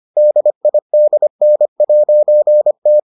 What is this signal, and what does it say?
Morse code, heard as a keyed tone, about 25 wpm, 600 Hz. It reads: DIDN'T